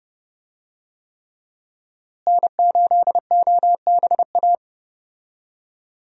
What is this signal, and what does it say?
Morse code, heard as a keyed tone, about 30 wpm, 700 Hz. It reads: N8OBA
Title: USA Call Signs - 5 Characters - Repeated 3x - 30wpm